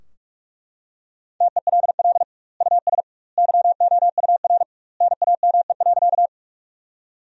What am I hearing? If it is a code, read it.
Morse code, heard as a keyed tone, about 42 wpm, 700 Hz. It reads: TELL US YOUR NAME.
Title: 2x Word-Spacing - Sentences from Top 200 Words - 45wpm